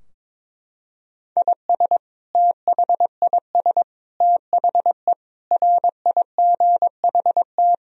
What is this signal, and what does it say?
Morse code, heard as a keyed tone, about 22 wpm, 700 Hz. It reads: IS THIS THE RIGHT
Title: Sentences from Top 1000 Words - Review Part 2 - 22wpm